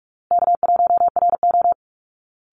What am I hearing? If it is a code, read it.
Morse code, heard as a keyed tone, about 44 wpm, 700 Hz. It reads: K1RO
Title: Easy Call Signs Repeated 3x - 45wpm